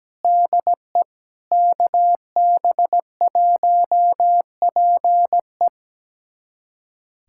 Morse code, heard as a keyed tone, about 17 wpm, 700 Hz. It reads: DE KB1PE
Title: USA Callsign DE Callsign 17wpm